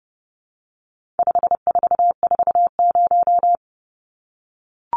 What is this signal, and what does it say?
Morse code, heard as a keyed tone, about 30 wpm, 700 Hz. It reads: 5440
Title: Numbers - 4-Digit 30wpm